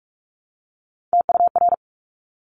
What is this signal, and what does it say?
Morse code, heard as a keyed tone, about 46 wpm, 700 Hz. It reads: TUR
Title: Top Three Letter Combinations in Words 45wpm